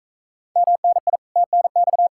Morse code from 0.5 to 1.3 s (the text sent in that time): MNI